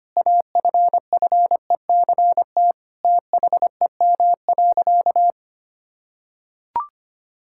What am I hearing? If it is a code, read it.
Morse code, heard as a keyed tone, about 25 wpm, 700 Hz. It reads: AFFECT THEM.
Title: Sentences from Top 1000 Words - Review Part 6 - 25wpm